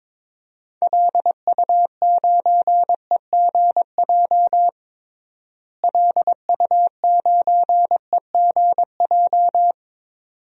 Morse code, heard as a keyed tone, about 22 wpm, 700 Hz. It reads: LU9EGJ LU9EGJ
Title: DXCC-100 Call Signs - 6 Characters - Repeated 2x - 22wpm